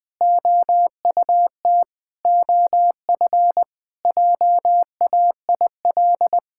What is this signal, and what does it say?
Morse code, heard as a keyed tone, about 20 wpm, 700 Hz. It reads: OUT OF JAIL
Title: Sentences with 7-Word Phrases - Spoken Beginning - 20wpm